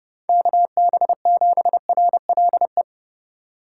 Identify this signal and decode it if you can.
Morse code, heard as a keyed tone, about 30 wpm, 700 Hz. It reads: KB7RLE
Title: USA Call Signs - 4x Character Spacing - 30wpm